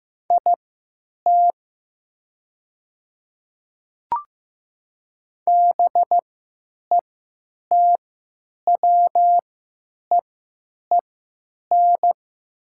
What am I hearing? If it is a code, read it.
Morse code, heard as a keyed tone, about 15 wpm, spaced as for 9 wpm, 700 Hz. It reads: IT BETWEEN
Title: 3x Character Spacing - Top 500 Words - 15wpm